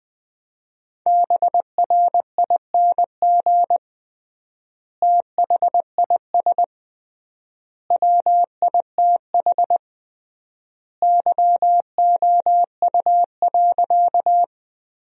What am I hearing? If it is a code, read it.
Morse code, heard as a keyed tone, about 20 wpm, 700 Hz. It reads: BRING THIS WITH YOU.
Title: 3x Word-Spacing - Sentences from Top 500 Words - 20wpm